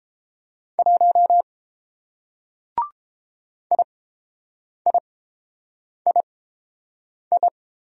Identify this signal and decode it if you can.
Morse code, heard as a keyed tone, about 29 wpm, spaced as for 7 wpm, 700 Hz. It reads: 1 IIII